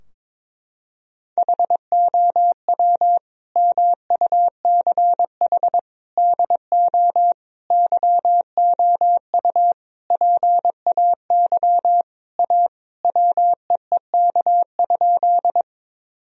Morse code, heard as a keyed tone, about 22 wpm, 700 Hz. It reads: HOW MUCH DO YOU PAY A WEEK?
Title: Sentences from Top 1000 Words - Review Part 2 - 22wpm